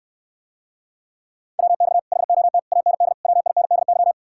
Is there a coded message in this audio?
Yes: Morse code from 1.6 to 4.1 s, heard as a keyed tone, about 69 wpm, 700 Hz. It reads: WX HOT AND WINDY